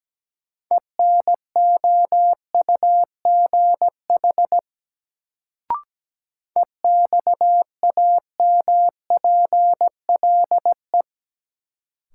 Morse code, heard as a keyed tone, about 17 wpm, 700 Hz. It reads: ENOUGH EXAMPLE
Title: Sentences from Top 400 Words - New Word Drill - 17wpm